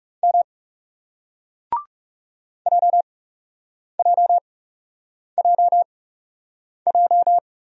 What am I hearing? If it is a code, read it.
Morse code, heard as a keyed tone, about 39 wpm, spaced as for 8 wpm, 700 Hz. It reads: M JJJJ